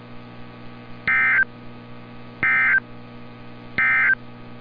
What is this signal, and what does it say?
Emergency Alert System (SAME) tones, heard as harsh buzzing bursts. It is heard from 1.0 to 4.2 s.
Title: NOAA Weather Radio - Altoona, PA